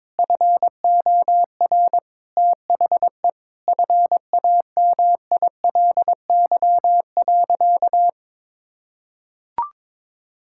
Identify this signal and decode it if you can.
Morse code, heard as a keyed tone, about 22 wpm, 700 Hz. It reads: FOR THE FAMILY.
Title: Sentences from Top 1000 Words - Review Part 5 - 22wpm